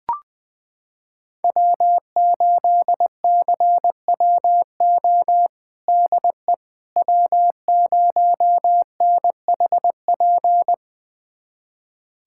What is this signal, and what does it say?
Morse code, heard as a keyed tone, about 20 wpm, 700 Hz. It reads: W8CWO DE W0NHP
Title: USA Callsign DE Callsign - 5 Characters - 20wpm